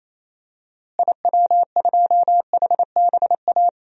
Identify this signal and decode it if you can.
Morse code, heard as a keyed tone, about 28 wpm, 700 Hz. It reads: IW2HBA
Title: DXCC-100 Call Signs - 6 Characters - Repeated 3x - 28wpm